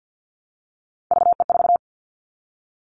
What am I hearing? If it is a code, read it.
Morse code, heard as a keyed tone, about 50 wpm, 700 Hz. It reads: VE4
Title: Callsign Prefixes for Top 50 DXCC - 50wpm